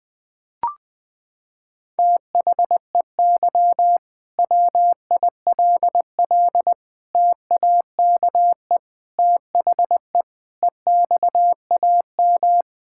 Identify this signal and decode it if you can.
Morse code, heard as a keyed tone, about 20 wpm, 700 Hz. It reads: THEY WILL TAKE THE EXAM
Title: Sentences from Top 1000 Words - Review Part 5 - 20wpm